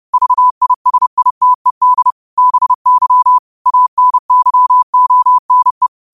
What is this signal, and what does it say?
Morse code, heard as a keyed tone, about 30 wpm, 1000 Hz. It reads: VISITED BY ANYONE
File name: qotdfiles / 30wpm